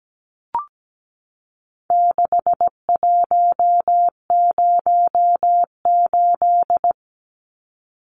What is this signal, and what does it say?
Morse code, heard as a keyed tone, about 17 wpm, 700 Hz. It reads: 6108
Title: Numbers - 4-Digit 17wpm